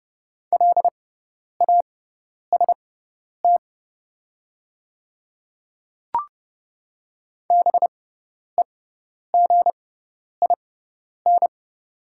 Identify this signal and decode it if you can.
Morse code, heard as a keyed tone, about 31 wpm, spaced as for 10 wpm, 700 Hz. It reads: LAST BEGIN
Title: 6x Character Spacing - Top 300 Words - 30wpm